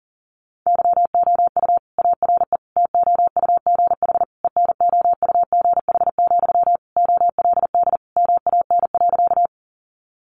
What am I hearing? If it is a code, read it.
Morse code, heard as a keyed tone, about 40 wpm, 700 Hz. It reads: YOU ARE TOUGH ENOUGH, OLD MAN.
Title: Sentences from Top 1200 Words 40wpm